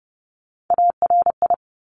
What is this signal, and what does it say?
Morse code, heard as a keyed tone, about 30 wpm, 700 Hz. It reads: ARI
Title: Top Three Letter Combinations in Words 30wpm